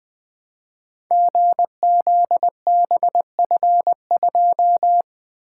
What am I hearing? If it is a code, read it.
Morse code, heard as a keyed tone, about 20 wpm, 700 Hz. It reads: GZBF2
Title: Code Groups - 5 Letter:Numbers - 20wpm